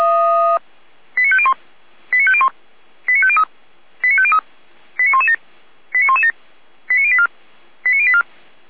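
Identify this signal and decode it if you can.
Five-tone selective call (ZVEI), heard as quick runs of five tones, tones of about 70 ms. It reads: 89501 89501 89503 89503 89117 89117 89084 89084
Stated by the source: ZVEI five-tone call 89104 received by SDR, from the feuerwehr-orlamuende.org fire brigade audio archive.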